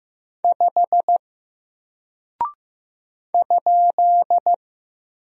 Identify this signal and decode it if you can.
Morse code, heard as a keyed tone, about 15 wpm, 700 Hz. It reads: 5 ?
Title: Single Character (Letter:Number:Punctuation:Prosign) 15wpm